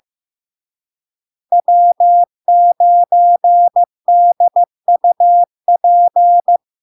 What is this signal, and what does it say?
Morse code, heard as a keyed tone, about 15 wpm, 700 Hz. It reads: W9DUP
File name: LARK-FD-2022-Log-in-CW-15wpm-700-Hz